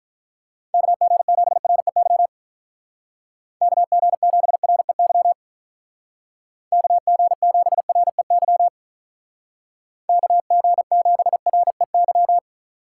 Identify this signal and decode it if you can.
Morse code, heard as a keyed tone, about 43 wpm, 700 Hz. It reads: KG7REY KG7REY KG7REY KG7REY
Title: Speed Racing - USA 2x3 Call Signs 35wpm